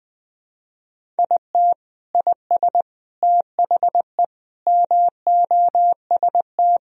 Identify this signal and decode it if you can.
Morse code, heard as a keyed tone, about 21 wpm, 700 Hz. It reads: IT IS THE MOST
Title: Sentences from Top 1000 Words - Review Part 1 - 20wpm